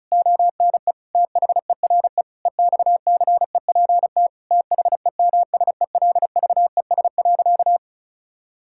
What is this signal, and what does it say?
Morse code, heard as a keyed tone, about 36 wpm, 700 Hz. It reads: ONE THERE EXCEPT THEMSELVES.
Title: Sentences from Top 1600 Words 35wpm